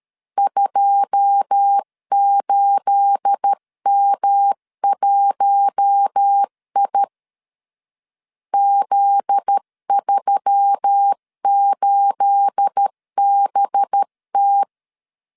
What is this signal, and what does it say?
Morse code, heard as a keyed tone, about 13 wpm, 800 Hz. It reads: 28M1I Z38BT
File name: LEZIONI / 9 28 12wpm 2351